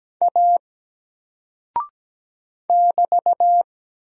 Morse code, heard as a keyed tone, about 17 wpm, 700 Hz. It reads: A <BT>
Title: Single Character (Letter:Number:Punctuation:Prosign) 17wpm